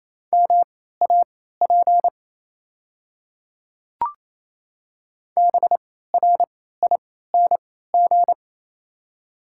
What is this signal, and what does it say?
Morse code, heard as a keyed tone, about 27 wpm, spaced as for 16 wpm, 700 Hz. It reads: MAP BRING
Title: 3x Character Spacing - Top 500 Words - 28wpm